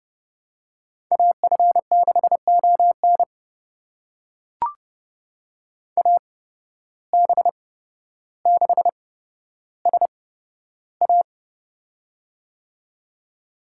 Morse code, heard as a keyed tone, about 30 wpm, spaced as for 13 wpm, 700 Hz. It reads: AF6ON AB6SA
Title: USA Call Signs - 5 Characters - 8x Character Spacing - 30wpm